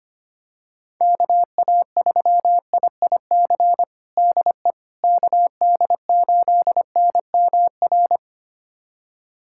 Morse code, heard as a keyed tone, about 25 wpm, 700 Hz. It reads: KA3IIC DE KD8NMR
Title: USA Callsign DE Callsign - 6 Characters - 25wpm